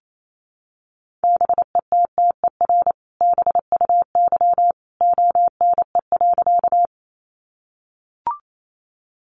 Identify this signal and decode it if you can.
Morse code, heard as a keyed tone, about 28 wpm, 700 Hz. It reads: BETTER BUY ONE.
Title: Sentences from Top 700 Words 28wpm